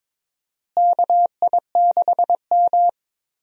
Morse code, heard as a keyed tone, about 22 wpm, 700 Hz. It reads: KI6M
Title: USA Call Signs - 4 Characters - Repeated 2x - 22wpm